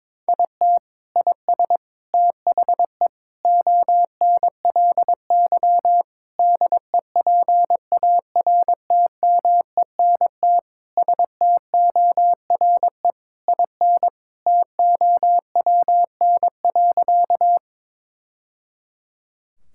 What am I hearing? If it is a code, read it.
Morse code, heard as a keyed tone, about 22 wpm, 700 Hz. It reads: IT IS THE ONLY DEPARTMENT STORE IN TOWN.
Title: Sentences from Top 1000 Words - Review Part 5 - 22wpm